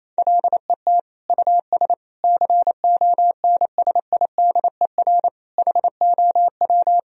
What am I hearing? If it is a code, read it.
Morse code, heard as a keyed tone, about 28 wpm, 700 Hz. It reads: LET US CONSIDER HOW
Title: Sentences from Top 1000 Words - Review Part 4 - 28wpm